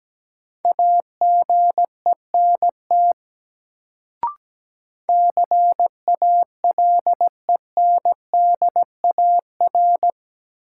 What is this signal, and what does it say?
Morse code, heard as a keyed tone, about 17 wpm, 700 Hz. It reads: AGENT CALENDAR
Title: Sentences from Top 1600 Words - New Word Drill - 17wpm